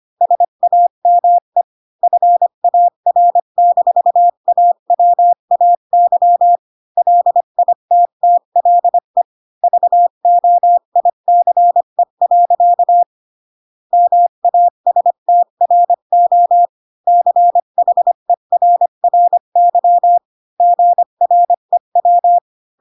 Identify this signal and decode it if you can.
Morse code, heard as a keyed tone, about 25 wpm, 700 Hz. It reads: SAME FAR-AWAY LITTLE VOICE. MASTRO CHERRY GREW
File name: Pinocchio / Pinocchio ch1 20 25 ik1hge